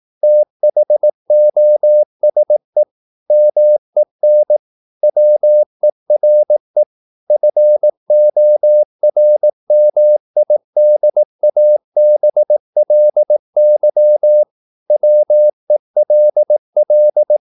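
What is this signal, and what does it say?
Morse code, heard as a keyed tone, about 18 wpm, 600 Hz. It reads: THOSE MEN WERE FORMIDABLY WELL
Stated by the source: Morse practice at 18 wpm, English text, from CW med Gnister.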